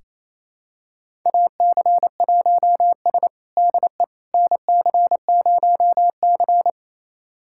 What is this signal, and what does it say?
Morse code, heard as a keyed tone, about 28 wpm, 700 Hz. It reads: AC1S DE NC0C
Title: USA Callsign DE Callsign - 4 Characters - 28wpm